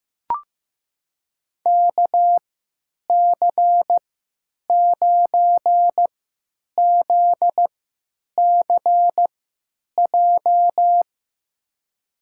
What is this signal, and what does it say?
Morse code, heard as a keyed tone, about 15 wpm, spaced as for 9 wpm, 700 Hz. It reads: KC9ZCJ
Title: USA Call Signs - 6 Characters - 3x Character Spacing - 15wpm